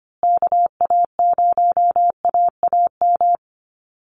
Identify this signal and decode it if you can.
Morse code, heard as a keyed tone, about 25 wpm, 700 Hz. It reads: KA0AAM
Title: Call Signs Repeated 2x - 25wpm